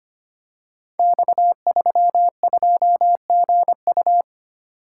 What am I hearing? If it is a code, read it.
Morse code, heard as a keyed tone, about 25 wpm, 700 Hz. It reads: X32GU
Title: Code Groups - 5 Letter:Numbers - 25wpm